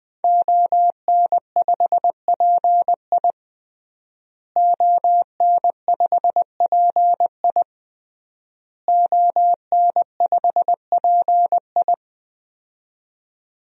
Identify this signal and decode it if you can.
Morse code, heard as a keyed tone, about 20 wpm, 700 Hz. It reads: ON5PI ON5PI ON5PI
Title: DXCC-100 Call Signs - 5 Characters - Repeated 3x - 20wpm